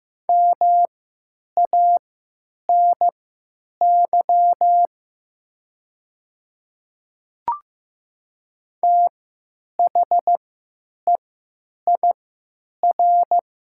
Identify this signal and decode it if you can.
Morse code, heard as a keyed tone, about 15 wpm, spaced as for 9 wpm, 700 Hz. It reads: MANY THEIR
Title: 3x Character Spacing - Top 100 Words - 15wpm